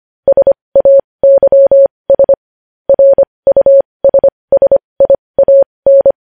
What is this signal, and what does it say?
Morse code, heard as a keyed tone, about 25 wpm, spaced as for 20 wpm, 550 Hz. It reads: SAYS RUSSIAN